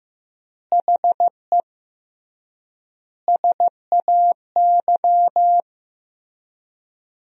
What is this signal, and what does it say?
Morse code, heard as a keyed tone, about 15 wpm, 700 Hz. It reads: HE SAY
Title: 3x Word-Spacing - Sentences from Top 300 Words - 15wpm